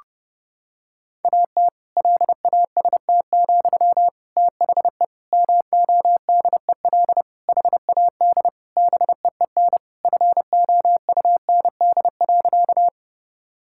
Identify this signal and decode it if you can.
Morse code, heard as a keyed tone, about 29 wpm, 700 Hz. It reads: AT LAST, THE MODEL HAD BEEN FOUND.
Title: Sentences from Top 1000 Words - Review Part 5 - 30wpm